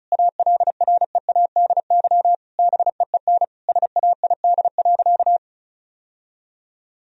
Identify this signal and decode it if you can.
Morse code, heard as a keyed tone, about 34 wpm, 700 Hz. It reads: ALREADY BEEN SAID.
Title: Sentences from Top 1000 Words - Review Part 1 - 35wpm